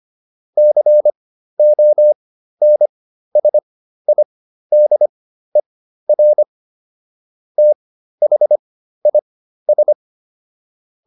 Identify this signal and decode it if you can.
Morse code, heard as a keyed tone, about 25 wpm, spaced as for 13 wpm, 600 Hz. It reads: CONSIDER THIS